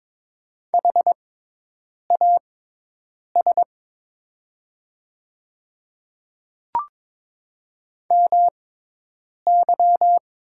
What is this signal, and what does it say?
Morse code, heard as a keyed tone, about 22 wpm, spaced as for 8 wpm, 700 Hz. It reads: HAS MY
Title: 6x Character Spacing - Top 300 Words - 22wpm